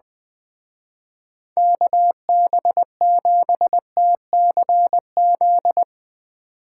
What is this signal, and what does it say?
Morse code, heard as a keyed tone, about 22 wpm, 700 Hz. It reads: KB7TCZ
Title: Speed Racing - USA 2x3 Call Signs 20wpm